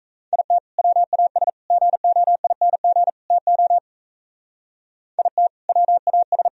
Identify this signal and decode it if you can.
Morse code, heard as a keyed tone, about 37 wpm, 700 Hz. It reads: IT WAS GOING TO IT WAS